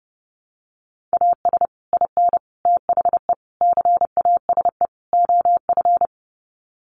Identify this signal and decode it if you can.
Morse code, heard as a keyed tone, about 30 wpm, 700 Hz. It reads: AS IN THE CASE OF
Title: Sets of 5 Words 30wpm